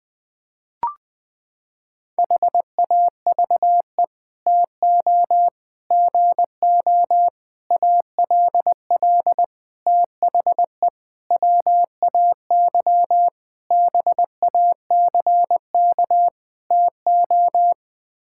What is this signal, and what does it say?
Morse code, heard as a keyed tone, about 20 wpm, 700 Hz. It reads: HAVE TO GO ALL THE WAY BACK TO
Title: Sets of 8 Words - 20wpm